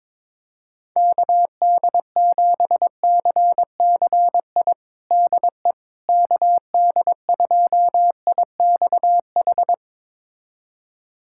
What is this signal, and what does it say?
Morse code, heard as a keyed tone, about 22 wpm, 700 Hz. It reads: KD7CCI DE KD2IXH
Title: USA Callsign DE Callsign - 6 Characters - 22wpm